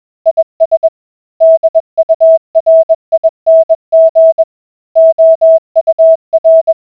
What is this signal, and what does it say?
Morse code, heard as a keyed tone, about 21 wpm, 650 Hz. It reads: IS DURING OUR